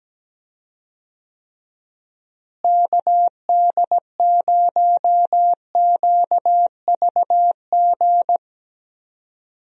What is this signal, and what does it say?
Morse code, heard as a keyed tone, about 17 wpm, 700 Hz. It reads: KD0QVG
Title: USA Call Signs - 6 Characters - Repeated 3x - 17wpm